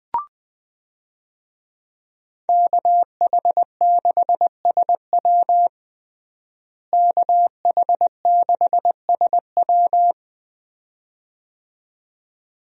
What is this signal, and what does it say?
Morse code, heard as a keyed tone, about 20 wpm, 700 Hz. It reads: KH6SW KH6SW
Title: DXCC-100 Call Signs - 5 Characters - Repeated 2x - 20wpm